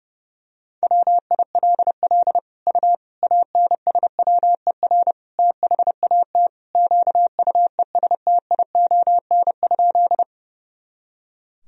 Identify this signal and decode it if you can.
Morse code, heard as a keyed tone, about 30 wpm, 700 Hz. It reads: WILL U ANSWER THAT QUESTION?
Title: Sentences from Top 400 Words and Abbreviations 30wpm